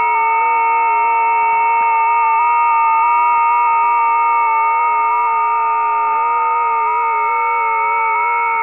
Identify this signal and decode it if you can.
FT8.